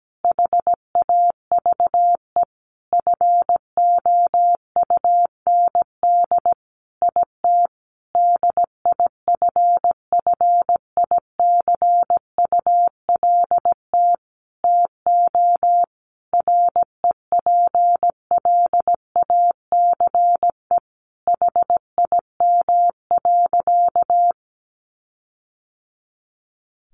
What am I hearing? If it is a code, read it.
Morse code, heard as a keyed tone, about 17 wpm, 700 Hz. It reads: HAVE FOUND IT DIFFICULT TO REPLACE HIM.
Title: Sentences from Top 1400 Words 17wpm